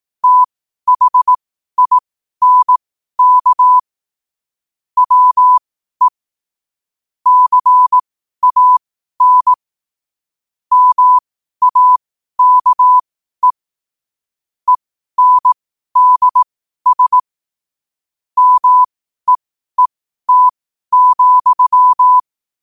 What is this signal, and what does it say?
Morse code, heard as a keyed tone, about 18 wpm, spaced as for 12 wpm, 1000 Hz. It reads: THINK WE CAN MAKE ENDS MEET,
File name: qotdfiles / 12wpm